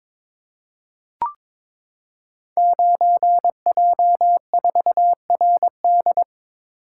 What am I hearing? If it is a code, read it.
Morse code, heard as a keyed tone, about 22 wpm, 700 Hz. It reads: 9J4RD
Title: Code Groups - 5 Letter:Numbers - 22wpm